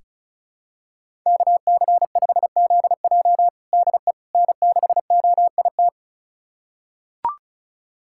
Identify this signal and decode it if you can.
Morse code, heard as a keyed tone, about 36 wpm, 700 Hz. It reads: KC5ZJ DE N6OIT
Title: USA Callsign DE Callsign - 5 Characters - 35wpm